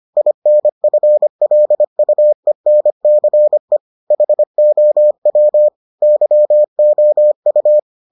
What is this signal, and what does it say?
Morse code, heard as a keyed tone, about 25 wpm, 600 Hz. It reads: INFLUENCE HOW YOU